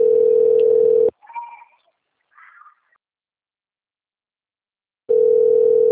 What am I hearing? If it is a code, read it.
Ringback tone (North America), 440 + 480 Hz.
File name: rg-1600-unknown- - - .38529